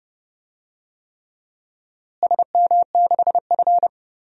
Morse code, heard as a keyed tone, about 29 wpm, 700 Hz. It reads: SM6F
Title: DXCC-100 Call Signs - 4 Characters - Repeated 3x - 30wpm